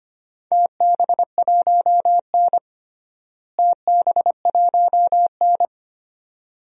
Morse code, heard as a keyed tone, about 25 wpm, 700 Hz. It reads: TB1N TB1N
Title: DXCC-100 Call Signs - 4 Characters - Repeated 2x - 25wpm